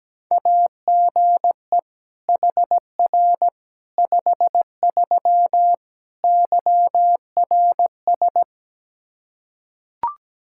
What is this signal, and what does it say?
Morse code, heard as a keyed tone, about 17 wpm, 700 Hz. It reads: AGE HR 53 YRS
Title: Practice QSO - Exchange 3 - More Info - 17wpm